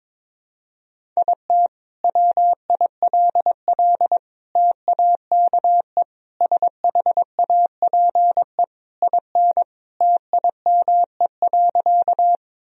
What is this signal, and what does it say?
Morse code, heard as a keyed tone, about 22 wpm, 700 Hz. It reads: IT WILL TAKE SHAPE IN TIME.
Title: Sentences from Top 1000 Words - Review Part 3 - 22wpm